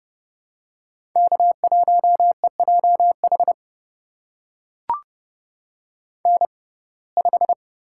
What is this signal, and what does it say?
Morse code, heard as a keyed tone, about 31 wpm, spaced as for 22 wpm, 700 Hz. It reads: K1EJH N5
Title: USA Call Signs - 5 Characters - 6x Character Spacing - 30wpm